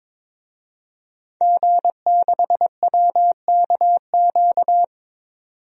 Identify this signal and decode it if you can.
Morse code, heard as a keyed tone, about 22 wpm, 700 Hz. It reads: G6WKQ
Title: DXCC-100 Call Signs - 5 Characters - 22wpm